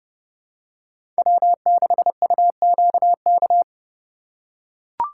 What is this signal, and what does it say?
Morse code, heard as a keyed tone, about 30 wpm, 700 Hz. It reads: W6UQK
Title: USA Call Signs - 4x Character Spacing - 30wpm